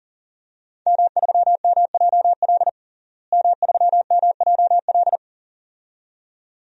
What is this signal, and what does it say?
Morse code, heard as a keyed tone, about 37 wpm, 700 Hz. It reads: M3MJL M3MJL
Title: DXCC-100 Call Signs - 5 Characters - Repeated 2x - 40wpm